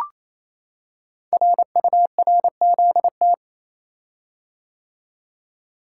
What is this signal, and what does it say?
Morse code, heard as a keyed tone, about 29 wpm, 700 Hz. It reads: RURZT
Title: Code Groups - 5 Letters - 28wpm